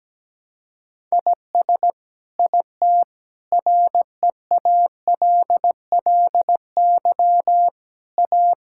Morse code, heard as a keyed tone, about 17 wpm, 700 Hz. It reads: IS IT REALLY A
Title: Sentences from Top 1000 Words - Review Part 5 - 17wpm